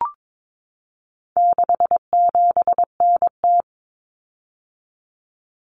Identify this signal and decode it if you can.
Morse code, heard as a keyed tone, about 22 wpm, 700 Hz. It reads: 67NT
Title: Common Cut Numbers - 4-Digit 22wpm